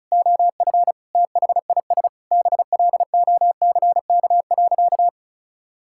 Morse code, heard as a keyed tone, about 34 wpm, 700 Hz. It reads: OF THIS BLOCK.
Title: Sentences from Top 1600 Words 35wpm